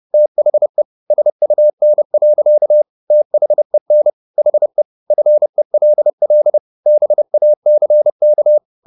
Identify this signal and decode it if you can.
Morse code, heard as a keyed tone, about 30 wpm, 600 Hz. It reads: THE SUN. THEN HE FELL BACK